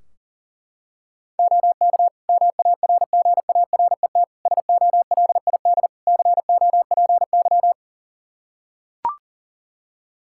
Morse code, heard as a keyed tone, about 37 wpm, 700 Hz. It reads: OK MARGARET SOLID COPY
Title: Practice QSO - Exchange 3 - More Info - 40wpm